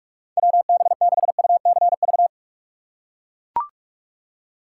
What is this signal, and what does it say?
Morse code, heard as a keyed tone, about 44 wpm, 700 Hz. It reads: WB6UCV
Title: USA Call Signs - 6 Characters - 6x Character Spacing - 45wpm